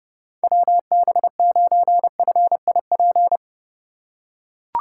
Morse code, heard as a keyed tone, about 32 wpm, 700 Hz. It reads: WB9FIP
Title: K1USN SST QSO Practice - 30wpm